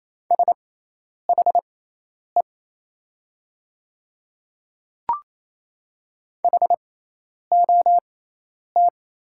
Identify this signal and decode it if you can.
Morse code, heard as a keyed tone, about 28 wpm, spaced as for 10 wpm, 690 Hz. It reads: SHE HOT
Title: 6x Character Spacing - Top 200 Words - 28wpm